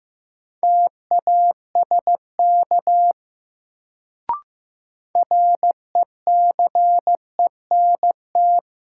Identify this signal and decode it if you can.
Morse code, heard as a keyed tone, about 15 wpm, 700 Hz. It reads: TASK RECENT
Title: Sentences from Top 1000 Words - Review Part 7 - New Word Drill - 15wpm